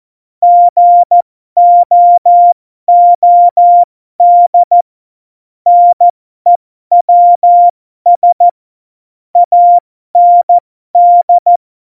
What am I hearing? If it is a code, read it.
Morse code, heard as a keyed tone, about 14 wpm, spaced as for 12 wpm, 700 Hz. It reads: GOOD NEWS AND
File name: Morsekurs / Witze